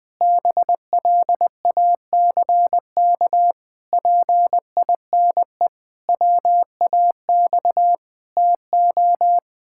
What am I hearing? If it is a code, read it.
Morse code, heard as a keyed tone, about 20 wpm, 700 Hz. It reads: BLACK PINE WAX TO
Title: Pangram Challenge v2 - 20wpm